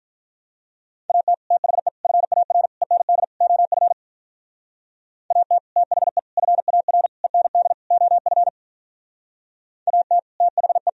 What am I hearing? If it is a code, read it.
Morse code, heard as a keyed tone, about 45 wpm, 700 Hz. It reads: AT THE FAR END OF AT THE FAR END OF AT THE